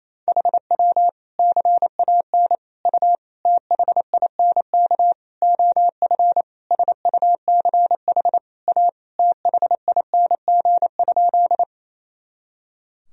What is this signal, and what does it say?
Morse code, heard as a keyed tone, about 28 wpm, 700 Hz. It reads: HW CAN U THINK OF SUCH A THING?
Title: Sentences from Top 200 Words and Abbreviations 28wpm